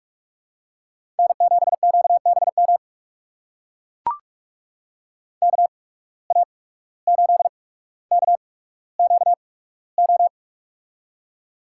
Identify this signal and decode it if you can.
Morse code, heard as a keyed tone, about 44 wpm, spaced as for 18 wpm, 700 Hz. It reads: N7QBM KA8KQO
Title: USA Call Signs - 8x Character Spacing - 45wpm